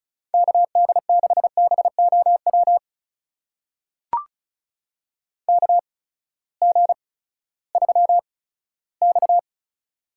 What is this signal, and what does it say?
Morse code, heard as a keyed tone, about 34 wpm, spaced as for 17 wpm, 700 Hz. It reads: KD6BOW KG3X
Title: USA Call Signs - 8x Character Spacing - 35wpm